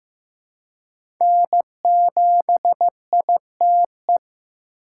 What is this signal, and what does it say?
Morse code, heard as a keyed tone, about 15 wpm, 700 Hz. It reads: N7ITE
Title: USA Call Signs - 5 Characters - 6x Character Spacing - 15wpm